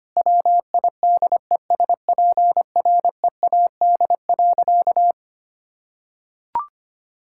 Morse code, heard as a keyed tone, about 25 wpm, 700 Hz. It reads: WIDESPREAD.
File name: practice / 25wpm / Sentences with 3-Word Phrases - 25wpm